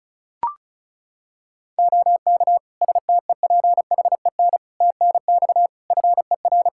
Morse code, heard as a keyed tone, about 34 wpm, 700 Hz. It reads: OK STEPHEN TNX FER
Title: Practice QSO - Exchange 4 - Ending - 35wpm